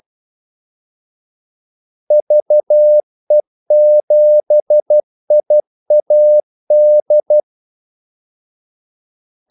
Morse code, heard as a keyed tone, about 12 wpm, 600 Hz. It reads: VE7IAD
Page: 2 ekstra ordmellemrum mellem kaldesignalerne